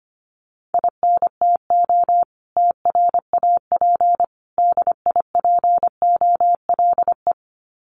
Morse code, heard as a keyed tone, about 25 wpm, 700 Hz. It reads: INTO TRAP DIPOLE
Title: Common QSO Phrases 25wpm